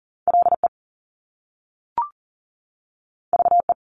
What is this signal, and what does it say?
Morse code, heard as a keyed tone, about 41 wpm, 700 Hz. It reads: LE VE
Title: Top Two Letter Combinations in Words 40wpm